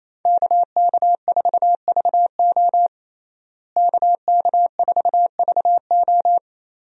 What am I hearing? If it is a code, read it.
Morse code, heard as a keyed tone, about 28 wpm, 700 Hz. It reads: KK4VO KK4VO
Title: USA Call Signs - 5 Characters - Repeated 2x - 28wpm